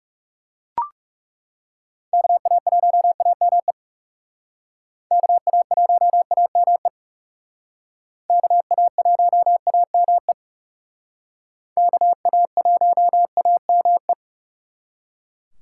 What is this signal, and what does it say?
Morse code, heard as a keyed tone, about 36 wpm, 700 Hz. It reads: KA1AME KA1AME KA1AME KA1AME
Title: Speed Racing - USA 2x3 Call Signs 30wpm